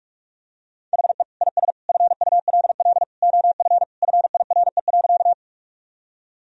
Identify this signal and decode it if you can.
Morse code, heard as a keyed tone, about 46 wpm, 700 Hz. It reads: HE IS FULL OF FIRE.
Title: Sentences from Top 1000 Words - Review Part 3 - 45wpm